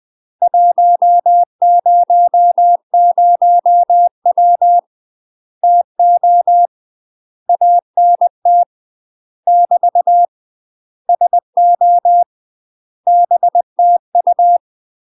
Morse code, heard as a keyed tone, about 20 wpm, 700 Hz. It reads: 100W TO ANT <BT> SO BTU
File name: UK MASTER QSOs TRAINING / 20WPM / UK MASTER QSOs 0031